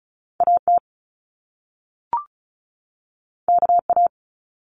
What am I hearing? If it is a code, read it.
Morse code, heard as a keyed tone, about 36 wpm, 700 Hz. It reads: AT KA
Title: Top Two Letter Combinations in Words 35wpm